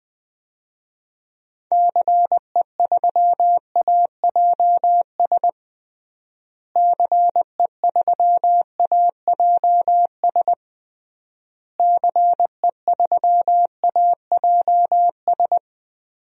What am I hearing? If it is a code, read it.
Morse code, heard as a keyed tone, about 20 wpm, 700 Hz. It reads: CE3AJS CE3AJS CE3AJS
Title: DXCC-100 Call Signs - 6 Characters - Repeated 3x - 20wpm